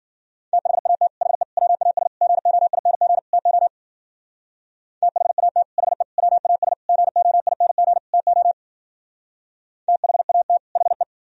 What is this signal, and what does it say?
Morse code, heard as a keyed tone, about 59 wpm, 700 Hz. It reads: THAT HE WAS GOING TO THAT HE WAS GOING TO THAT HE